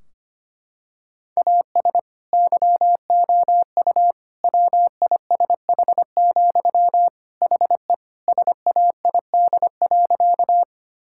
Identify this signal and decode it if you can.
Morse code, heard as a keyed tone, about 25 wpm, 700 Hz. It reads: AS YOU WISH, HE SAID.
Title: Sentences from Top 1600 Words 25wpm